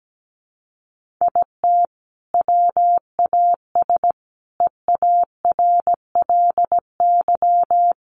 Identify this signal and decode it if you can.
Morse code, heard as a keyed tone, about 17 wpm, 700 Hz. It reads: IT WAS EARLY
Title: Sentences from Top 1500 Words 17wpm